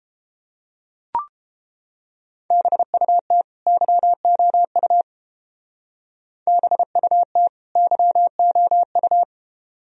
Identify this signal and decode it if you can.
Morse code, heard as a keyed tone, about 30 wpm, 700 Hz. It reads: BUT YOU BUT YOU